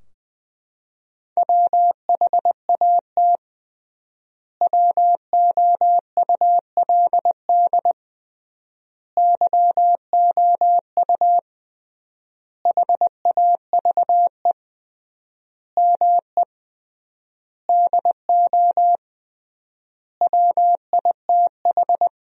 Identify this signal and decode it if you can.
Morse code, heard as a keyed tone, about 20 wpm, 700 Hz. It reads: WHAT WOULD YOU HAVE ME DO WITH
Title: 3x Word-Spacing - Sentences from Top 300 Words - 20wpm